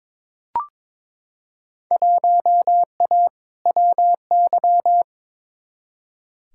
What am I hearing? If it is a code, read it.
Morse code, heard as a keyed tone, about 22 wpm, 700 Hz. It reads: 1A WY
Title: Field Day Exchanges - 22wpm